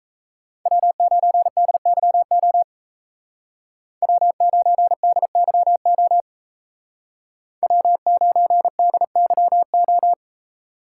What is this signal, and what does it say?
Morse code, heard as a keyed tone, about 36 wpm, 700 Hz. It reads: W9DYO W9DYO W9DYO